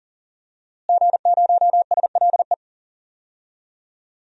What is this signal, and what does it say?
Morse code, heard as a keyed tone, about 40 wpm, 700 Hz. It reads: G0SLE
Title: DXCC-100 Call Signs - 5 Characters - Repeated 3x - 40wpm